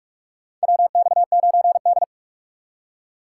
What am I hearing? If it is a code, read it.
Morse code, heard as a keyed tone, about 46 wpm, 700 Hz. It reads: WX9D
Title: USA Call Signs - 4 Characters - 6x Character Spacing - 45wpm